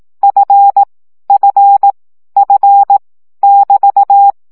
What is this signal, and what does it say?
Morse code, heard as a keyed tone, about 18 wpm, 800 Hz. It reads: F F F <BT>
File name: CW gyakorlas f 15 baud